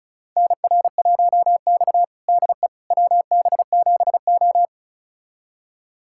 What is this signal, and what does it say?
Morse code, heard as a keyed tone, about 36 wpm, 700 Hz. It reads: NR1X DE WB7O
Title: USA Callsign DE Callsign - 4 Characters - 35wpm